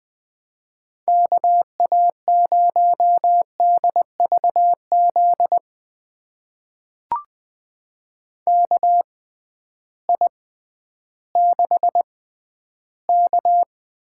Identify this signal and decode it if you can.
Morse code, heard as a keyed tone, about 21 wpm, spaced as for 12 wpm, 700 Hz. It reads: KA0DVZ KI6K
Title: USA Call Signs - 6 Characters - 6x Character Spacing - 20wpm